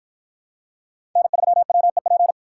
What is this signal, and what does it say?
Morse code, heard as a keyed tone, about 49 wpm, 700 Hz. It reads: N3WEP